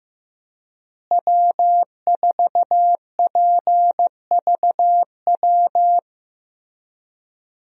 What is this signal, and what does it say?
Morse code, heard as a keyed tone, about 15 wpm, 700 Hz. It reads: W4PVW
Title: USA Call Signs - 5 Characters - Repeated 3x - 15wpm